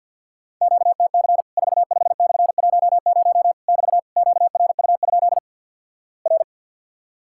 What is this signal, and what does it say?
Morse code, heard as a keyed tone, about 50 wpm, 700 Hz. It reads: QTC 45/10 <BT> QRU? R
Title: Worked All Europe DX Contest - QTC Exchange - 50wpm